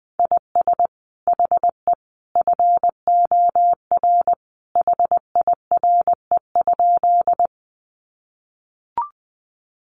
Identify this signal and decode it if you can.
Morse code, heard as a keyed tone, about 20 wpm, 700 Hz. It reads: IS HE FOR HIRE?
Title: Sentences from Top 1400 Words 20wpm